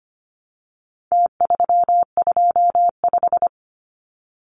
Morse code, heard as a keyed tone, about 25 wpm, 700 Hz. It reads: T325
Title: Numbers - Contest Style - 4-Digit 25wpm